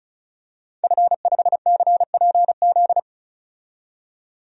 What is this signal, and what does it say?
Morse code, heard as a keyed tone, about 36 wpm, 700 Hz. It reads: F5CPZ
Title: DXCC-100 Call Signs - 5 Characters - 35wpm